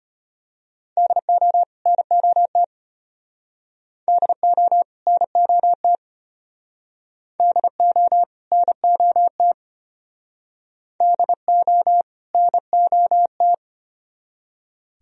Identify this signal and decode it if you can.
Morse code, heard as a keyed tone, about 32 wpm, 700 Hz. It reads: DO NOT DO NOT DO NOT DO NOT